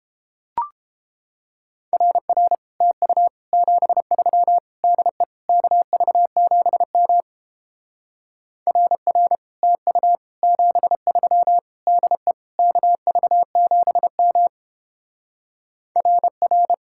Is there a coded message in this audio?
Yes: Morse code from 1.9 to 16.7 s, heard as a keyed tone, about 31 wpm, 700 Hz. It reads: RR TU 73 DE KV7M RR TU 73 DE KV7M RR